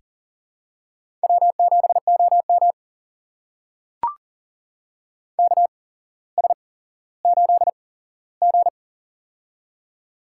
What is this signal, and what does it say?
Morse code, heard as a keyed tone, about 40 wpm, spaced as for 17 wpm, 700 Hz. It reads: W7OM KS8G
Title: USA Call Signs - 4 Characters - 8x Character Spacing - 40wpm